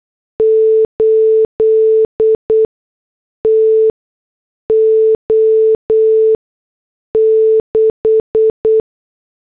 Morse code, heard as a keyed tone, about 8 wpm, spaced as for 6 wpm, 440 Hz. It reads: 8 TO 6